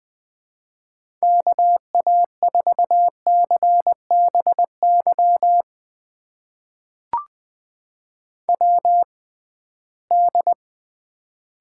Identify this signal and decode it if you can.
Morse code, heard as a keyed tone, about 20 wpm, spaced as for 15 wpm, 700 Hz. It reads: KA4CBY WD
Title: USA Call Signs - 6 Characters - 6x Character Spacing - 20wpm